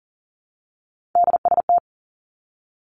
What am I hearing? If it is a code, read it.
Morse code, heard as a keyed tone, about 40 wpm, 700 Hz. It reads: DST
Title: Three Challenging Letter-Number Combinations 40wpm